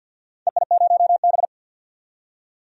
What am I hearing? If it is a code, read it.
Morse code, heard as a keyed tone, about 51 wpm, 700 Hz. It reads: EI0B
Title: DXCC-100 Call Signs - 4 Characters - 50wpm